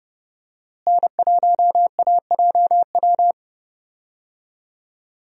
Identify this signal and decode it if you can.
Morse code, heard as a keyed tone, about 30 wpm, 700 Hz. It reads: N1AJW
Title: USA Call Signs - 5 Characters - Repeated 2x - 30wpm